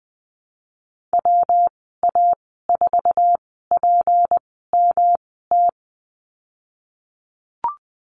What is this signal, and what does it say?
Morse code, heard as a keyed tone, about 20 wpm, spaced as for 15 wpm, 700 Hz. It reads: WA4PMT
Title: 167 Lesson 27g - Number 7 - Call Signs 20wpm